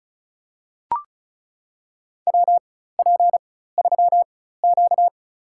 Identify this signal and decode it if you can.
Morse code, heard as a keyed tone, about 34 wpm, spaced as for 16 wpm, 700 Hz. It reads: WP3Q
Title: USA Call Signs - 4 Characters - 4x Character Spacing - 35wpm